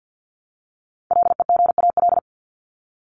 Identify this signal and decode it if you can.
Morse code, heard as a keyed tone, about 50 wpm, 700 Hz. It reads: LEGAL
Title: Top 1001-1250 Words 50wpm